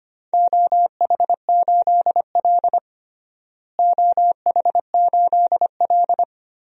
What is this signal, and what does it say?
Morse code, heard as a keyed tone, about 25 wpm, 700 Hz. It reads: OH8L OH8L
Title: DXCC-100 Call Signs - 4 Characters - Repeated 3x - 25wpm